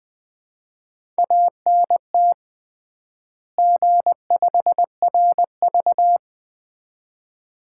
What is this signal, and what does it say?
Morse code, heard as a keyed tone, about 21 wpm, 700 Hz. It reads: ANT G5RV
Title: Common QSO Phrases - 3x Word-Spacing - 20wpm